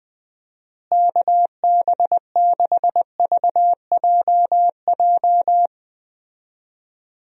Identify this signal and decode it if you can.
Morse code, heard as a keyed tone, about 20 wpm, 700 Hz. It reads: KB6VJJ
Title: USA Call Signs - 6 Characters - Repeated 3x - 20wpm